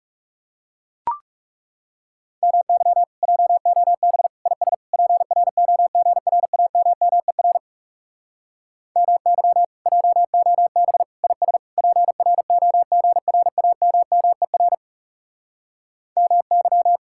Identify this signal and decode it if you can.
Morse code, heard as a keyed tone, about 40 wpm, 700 Hz. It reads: MY JOB IS PROGRAMMER MY JOB IS PROGRAMMER MY